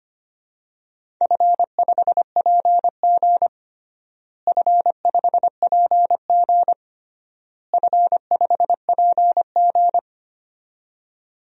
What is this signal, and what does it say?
Morse code, heard as a keyed tone, about 25 wpm, 700 Hz. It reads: F5PG F5PG F5PG
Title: DXCC-100 Call Signs - 4 Characters - Repeated 3x - 25wpm